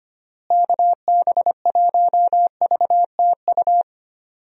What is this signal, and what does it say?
Morse code, heard as a keyed tone, about 25 wpm, 700 Hz. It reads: KB1VTU
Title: USA Call Signs - 6 Characters - Repeated 3x - 25wpm